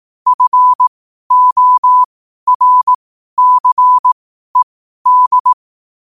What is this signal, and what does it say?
Morse code, heard as a keyed tone, about 18 wpm, spaced as for 12 wpm, 1000 Hz. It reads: FORCED
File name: qotdfiles / 12wpm